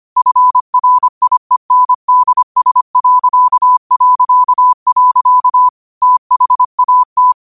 Morse code, heard as a keyed tone, about 25 wpm, 1000 Hz. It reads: FRIENDS... THAT
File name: qotdfiles / 25wpm